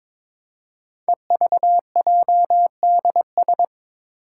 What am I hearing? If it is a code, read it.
Morse code, heard as a keyed tone, about 22 wpm, 700 Hz. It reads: EVJDS
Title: Code Groups - 5 Letters - 22wpm